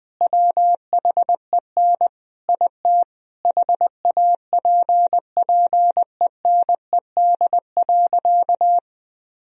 Morse code, heard as a keyed tone, about 20 wpm, 700 Hz. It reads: WHEN IT HAPPENED.
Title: Sentences from Top 1000 Words - Review Part 4 - 20wpm